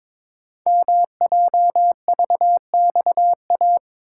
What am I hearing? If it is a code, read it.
Morse code, heard as a keyed tone, about 23 wpm, 700 Hz. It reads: MJVXA
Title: Code Groups - 5 Letters - 22wpm